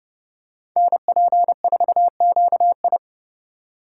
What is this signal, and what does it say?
Morse code, heard as a keyed tone, about 32 wpm, 700 Hz. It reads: NP4QI
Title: USA Call Signs - 5 Characters - 4x Character Spacing - 30wpm